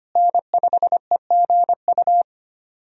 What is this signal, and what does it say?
Morse code, heard as a keyed tone, about 25 wpm, 700 Hz. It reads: N5EGU
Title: USA Call Signs - 5 Characters - Repeated 3x - 25wpm